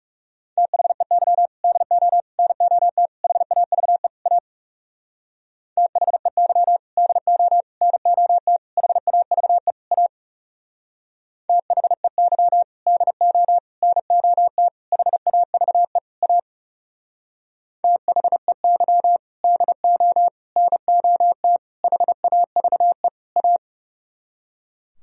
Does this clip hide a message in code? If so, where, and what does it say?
Yes: Morse code from 0.6 to 23.6 s, heard as a keyed tone, about 39 wpm, 700 Hz. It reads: THEY DO NOT HAVE A THEY DO NOT HAVE A THEY DO NOT HAVE A THEY DO NOT HAVE A